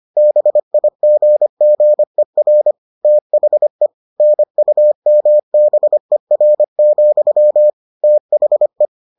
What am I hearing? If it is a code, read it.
Morse code, heard as a keyed tone, about 25 wpm, 600 Hz. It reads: BIGGER THE NUMBER, THE